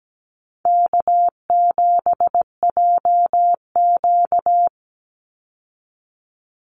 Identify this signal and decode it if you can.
Morse code, heard as a keyed tone, about 17 wpm, 700 Hz. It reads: K7JQ
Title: Easy Call Signs Repeated 3x - 17wpm